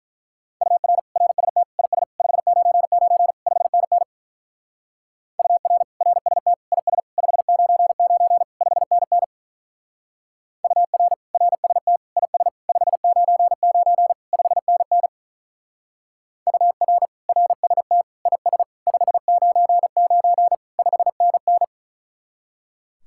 Morse code, heard as a keyed tone, about 45 wpm, 700 Hz. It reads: UR RST IS 599 5NN UR RST IS 599 5NN UR RST IS 599 5NN UR RST IS 599 5NN